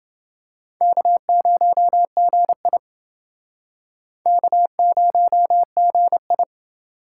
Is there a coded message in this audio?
Yes: Morse code from 0.8 to 6.4 s, heard as a keyed tone, about 28 wpm, 700 Hz. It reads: K0GI K0GI